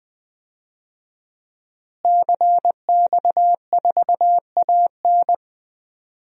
Morse code, heard as a keyed tone, about 20 wpm, 700 Hz. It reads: CX4AN
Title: DXCC-100 Call Signs - 5 Characters - Repeated 2x - 20wpm